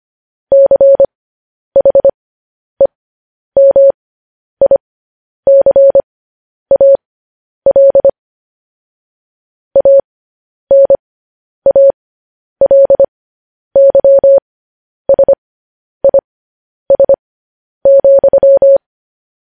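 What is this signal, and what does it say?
Morse code, heard as a keyed tone, about 25 wpm, spaced as for 10 wpm, 550 Hz. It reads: CHEMICAL ANALYSIS,
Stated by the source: Morse code practice of news headlines at 10 wpm.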